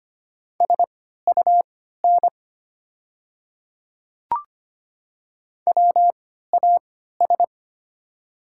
Morse code, heard as a keyed tone, about 25 wpm, spaced as for 14 wpm, 700 Hz. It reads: SUN WAS
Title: 3x Character Spacing - Top 300 Words - 25wpm